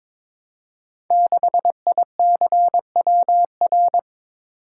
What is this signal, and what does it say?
Morse code, heard as a keyed tone, about 22 wpm, 700 Hz. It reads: 6ICWR
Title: Code Groups - 5 Letter:Numbers - 22wpm